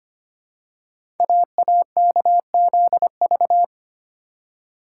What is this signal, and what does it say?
Morse code, heard as a keyed tone, about 25 wpm, 700 Hz. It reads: AAKZV
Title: Code Groups - 5 Letter:Numbers - 25wpm